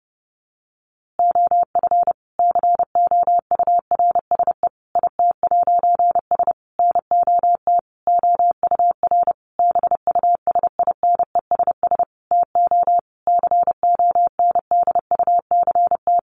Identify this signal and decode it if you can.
Morse code, heard as a keyed tone, about 28 wpm, 700 Hz. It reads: OF COURSE IT'S NOT OUR BUSINESS TO CONDUCT
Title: A Sound Of Thunder 30wpm